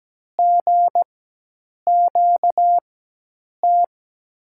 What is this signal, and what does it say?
Morse code, heard as a keyed tone, about 17 wpm, spaced as for 8 wpm, 700 Hz. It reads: GQT
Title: USA Call Signs - 6 Characters - 4x Character Spacing - 17wpm